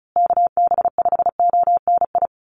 Morse code, heard as a keyed tone, about 34 wpm, 700 Hz. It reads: KB5ONI
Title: USA Call Signs Repeated 3x 35wpm